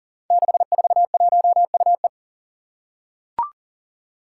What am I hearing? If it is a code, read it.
Morse code, heard as a keyed tone, about 39 wpm, 700 Hz. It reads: 641UE
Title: Five Challenging Letter-Number Combinations 40wpm